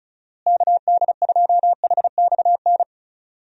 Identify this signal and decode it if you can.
Morse code, heard as a keyed tone, about 34 wpm, 700 Hz. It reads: KD2HXN
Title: USA Call Signs - 6 Characters - Repeated 3x - 35wpm